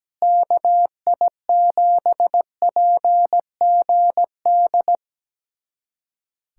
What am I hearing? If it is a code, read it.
Morse code, heard as a keyed tone, about 17 wpm, 700 Hz. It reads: KI7PGD
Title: USA Callsign DE Callsign 17wpm